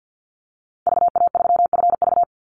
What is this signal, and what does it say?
Morse code, heard as a keyed tone, about 48 wpm, 700 Hz. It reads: VA3FV
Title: Call Signs Repeated 2x - 50wpm